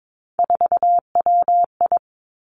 Morse code, heard as a keyed tone, about 22 wpm, 700 Hz. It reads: 4WI
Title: Three Challenging Letter-Number Combinations 22wpm